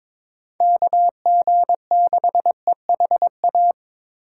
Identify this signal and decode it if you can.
Morse code, heard as a keyed tone, about 22 wpm, 700 Hz. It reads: KG6EHA
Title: USA Call Signs - 6 Characters - 4x Character Spacing - 22wpm